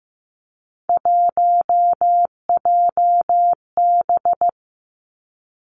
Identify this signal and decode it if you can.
Morse code, heard as a keyed tone, about 15 wpm, 700 Hz. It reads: 1JB
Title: Three Challenging Letter-Number Combinations 15wpm